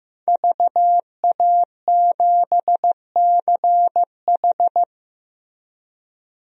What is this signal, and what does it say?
Morse code, heard as a keyed tone, about 16 wpm, 700 Hz. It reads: VA7CH
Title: DXCC-100 Call Signs - 5 Characters - Repeated 2x - 15wpm